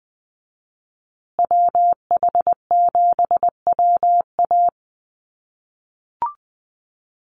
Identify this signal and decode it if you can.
Morse code, heard as a keyed tone, about 20 wpm, 700 Hz. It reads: WH7WA
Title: USA Call Signs Repeated 2x 20wpm